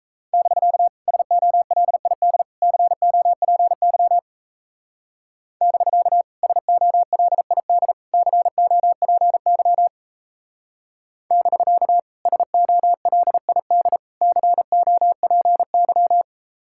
Morse code, heard as a keyed tone, about 34 wpm, 700 Hz. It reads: <BK> SOLID COPY <BK> SOLID COPY <BK> SOLID COPY
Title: Speed-Racing Common QSO Phrases Encore - 28wpm